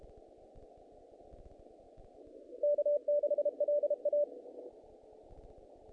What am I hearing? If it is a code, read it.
Morse code, heard as a keyed tone, about 32 wpm, 590 Hz. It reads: K6LA
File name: contests / naqp-cw-k5tr-op-k5pi- -2000